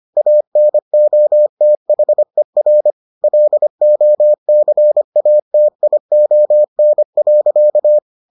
Morse code, heard as a keyed tone, about 25 wpm, 600 Hz. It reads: ANOTHER LOCATION.